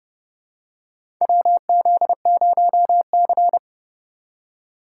Morse code, heard as a keyed tone, about 30 wpm, 700 Hz. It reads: WZ0C
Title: USA Call Signs - 2x Character Spacing - 30wpm